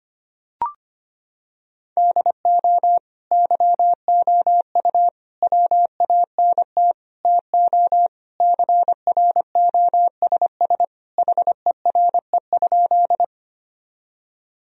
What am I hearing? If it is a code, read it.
Morse code, heard as a keyed tone, about 25 wpm, 700 Hz. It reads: DO YOU WANT TO CROSS HERE?
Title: Sentences from Top 1000 Words - Review Part 1 - 25wpm